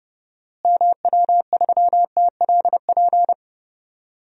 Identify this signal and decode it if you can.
Morse code, heard as a keyed tone, about 30 wpm, 700 Hz. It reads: MW3TLP
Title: DXCC-100 Call Signs - 6 Characters - Repeated 2x - 30wpm